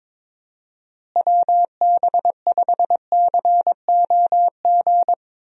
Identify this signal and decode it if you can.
Morse code, heard as a keyed tone, about 22 wpm, 700 Hz. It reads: WB5COG
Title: USA Call Signs - 6 Characters - Repeated 3x - 22wpm